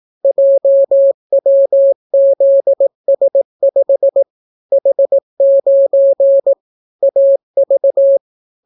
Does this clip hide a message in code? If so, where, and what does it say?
Yes: Morse code from 0.2 to 8.2 s, heard as a keyed tone, about 18 wpm, 550 Hz. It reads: JWZS5 H9 AV